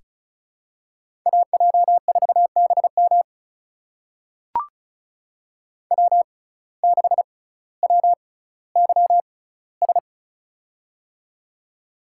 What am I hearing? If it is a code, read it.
Morse code, heard as a keyed tone, about 36 wpm, spaced as for 18 wpm, 700 Hz. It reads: AJ4BM W6WYS
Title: USA Call Signs - 5 Characters - 6x Character Spacing - 35wpm